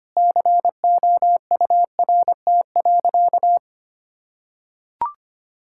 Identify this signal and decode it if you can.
Morse code, heard as a keyed tone, about 25 wpm, 700 Hz. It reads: COURT.
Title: Sentences from Top 1600 Words 25wpm